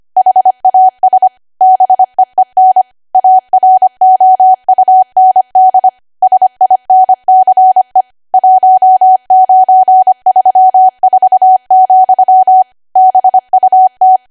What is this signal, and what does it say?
Morse code, heard as a keyed tone, about 25 wpm, 750 Hz. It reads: HAS BEEN AROUND SINCE 1934, BUT
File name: Archive / 25 WPM / 25WPM